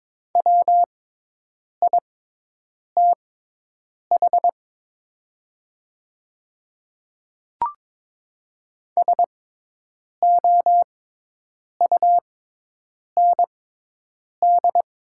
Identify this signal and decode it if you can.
Morse code, heard as a keyed tone, about 22 wpm, spaced as for 8 wpm, 700 Hz. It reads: WITH SOUND
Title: 6x Character Spacing - Top 100 Words - 22wpm